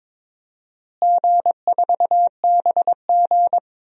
Morse code, heard as a keyed tone, about 22 wpm, 700 Hz. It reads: G4BG
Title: DXCC-100 Call Signs - 4 Characters - Repeated 2x - 22wpm